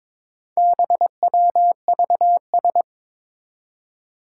Morse code, heard as a keyed tone, about 22 wpm, 700 Hz. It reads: BWVS
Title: Four Challenging Letter-Number Combinations 22wpm